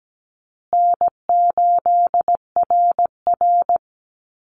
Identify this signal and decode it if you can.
Morse code, heard as a keyed tone, about 17 wpm, 700 Hz. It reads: N8RR
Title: Easy Call Signs Repeated 3x - 17wpm